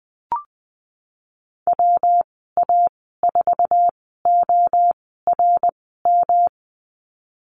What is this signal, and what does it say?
Morse code, heard as a keyed tone, about 19 wpm, spaced as for 14 wpm, 700 Hz. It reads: WA4ORM
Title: 161 Lesson 26g - Letter g - Call Signs 20wpm